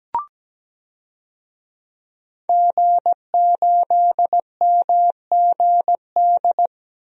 Morse code, heard as a keyed tone, about 17 wpm, 700 Hz. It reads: G8MGD
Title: DXCC-100 Call Signs - 5 Characters - Repeated 2x - 17wpm